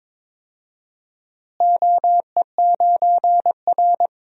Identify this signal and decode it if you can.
Morse code, heard as a keyed tone, about 21 wpm, 700 Hz. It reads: OE9R
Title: DXCC-100 Call Signs - 4 Characters - Repeated 3x - 22wpm